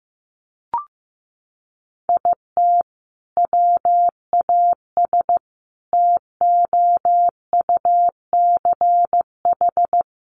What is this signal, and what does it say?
Morse code, heard as a keyed tone, about 15 wpm, 700 Hz. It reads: IT WAS TOUCH
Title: Sentences from Top 1200 Words 15wpm